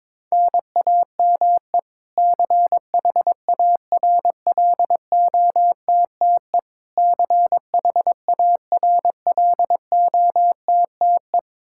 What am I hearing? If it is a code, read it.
Morse code, heard as a keyed tone, about 21 wpm, 700 Hz. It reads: NAME CHARLOTTE CHARLOTTE
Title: Practice QSO - Exchange 1 - Beginning - 22wpm